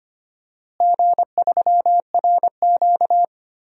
Morse code, heard as a keyed tone, about 25 wpm, 700 Hz. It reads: G3RQ
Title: DXCC-100 Call Signs - 4 Characters - Repeated 3x - 25wpm